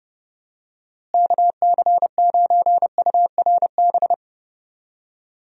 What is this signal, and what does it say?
Morse code, heard as a keyed tone, about 30 wpm, 700 Hz. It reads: KC9URB
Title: USA Call Signs - 6 Characters - Repeated 2x - 30wpm